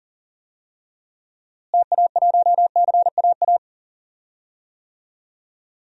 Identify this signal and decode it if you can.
Morse code, heard as a keyed tone, about 40 wpm, 700 Hz. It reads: TA1CAA
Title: DXCC-100 Call Signs - 6 Characters - Repeated 2x - 40wpm